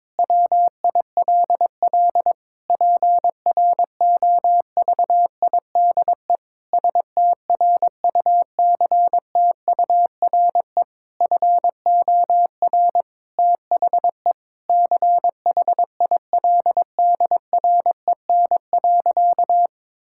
Morse code, heard as a keyed tone, about 21 wpm, 700 Hz. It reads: WILL PROVIDE STRUCTURE FOR THE CHILDREN.
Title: Sentences from Top 1000 Words - Review Part 6 - 22wpm